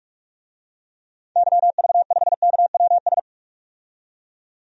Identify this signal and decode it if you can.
Morse code, heard as a keyed tone, about 46 wpm, 700 Hz. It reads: YV5KWS
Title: DXCC-100 Call Signs - 6 Characters - 45wpm